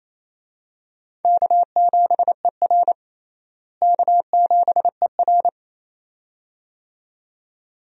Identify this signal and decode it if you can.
Morse code, heard as a keyed tone, about 28 wpm, 700 Hz. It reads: K7ER K7ER
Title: USA Call Signs - 4 Characters - Repeated 2x - 28wpm